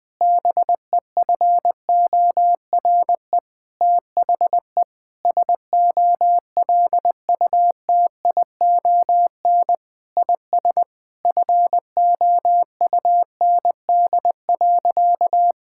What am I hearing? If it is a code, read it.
Morse code, heard as a keyed tone, about 20 wpm, 700 Hz. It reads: BEFORE THE SOLUTION IS FOUND.
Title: Sentences with 9-Word Phrases - 20wpm